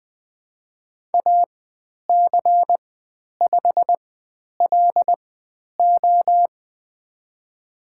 Morse code, heard as a keyed tone, about 20 wpm, spaced as for 10 wpm, 700 Hz. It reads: AC5LO
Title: 095 Lesson 16g - Letter c - Call Signs 20wpm